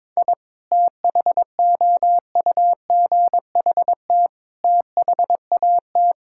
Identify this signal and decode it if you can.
Morse code, heard as a keyed tone, about 22 wpm, 700 Hz. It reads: I THOUGHT THAT
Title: Sentences from Top 1000 Words - Review Part 5 - 22wpm